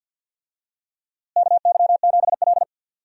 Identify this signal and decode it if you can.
Morse code, heard as a keyed tone, about 47 wpm, 700 Hz. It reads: KY7L
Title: USA Call Signs - 4 Characters - Repeated 2x - 50wpm